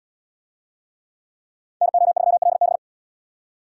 Morse code, heard as a keyed tone, about 69 wpm, 700 Hz. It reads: NO3LB